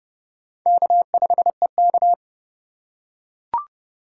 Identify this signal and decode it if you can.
Morse code, heard as a keyed tone, about 29 wpm, 700 Hz. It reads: K5EK